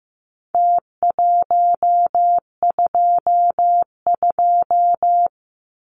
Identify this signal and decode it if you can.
Morse code, heard as a keyed tone, about 15 wpm, 700 Hz. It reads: T122
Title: Numbers - Contest Style - 4-Digit 15wpm